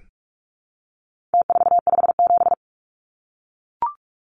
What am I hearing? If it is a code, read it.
Morse code, heard as a keyed tone, about 44 wpm, 700 Hz. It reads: T457
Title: Numbers - Contest Style - 4-Digit 45wpm